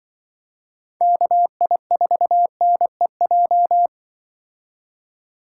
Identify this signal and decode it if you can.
Morse code, heard as a keyed tone, about 24 wpm, 700 Hz. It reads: KI4NEJ
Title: Speed Racing - USA Call Signs 20wpm